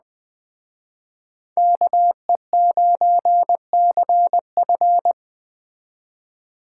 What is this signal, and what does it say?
Morse code, heard as a keyed tone, about 20 wpm, 700 Hz. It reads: KE9CF
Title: Speed Racing - USA Call Signs 17wpm